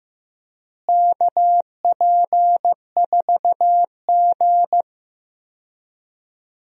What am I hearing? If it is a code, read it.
Morse code, heard as a keyed tone, about 15 wpm, 700 Hz. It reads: KP4G
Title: DXCC-100 Call Signs - 4 Characters - Repeated 3x - 15wpm